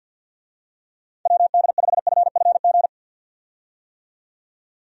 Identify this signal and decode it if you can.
Morse code, heard as a keyed tone, about 46 wpm, 700 Hz. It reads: WD5FFG
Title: USA Call Signs - 6 Characters - 50wpm